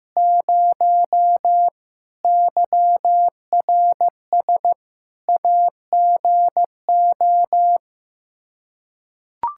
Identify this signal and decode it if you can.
Morse code, heard as a keyed tone, about 16 wpm, 700 Hz. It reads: 0 YRS AGO
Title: Practice QSO - Exchange 3 - More Info - 15wpm